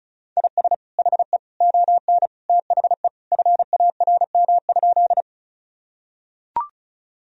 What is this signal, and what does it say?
Morse code, heard as a keyed tone, about 34 wpm, 700 Hz. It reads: IS HE ON THE FARM?
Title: Sentences from Top 1000 Words - Review Part 2 - 35wpm